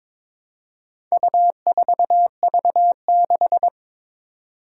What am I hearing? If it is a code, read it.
Morse code, heard as a keyed tone, about 22 wpm, 700 Hz. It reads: U4V6
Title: Four Challenging Letter-Number Combinations 22wpm